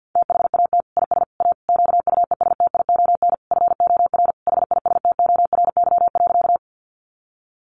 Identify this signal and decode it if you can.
Morse code, heard as a keyed tone, about 50 wpm, 700 Hz. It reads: THAT IS A QUESTION FOR HISTORY.
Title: Sentences from Top 600 Words 50wpm